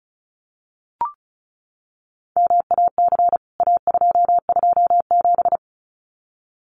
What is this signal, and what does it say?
Morse code, heard as a keyed tone, about 34 wpm, 700 Hz. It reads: MAC A227
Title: CWT Exchange 35wpm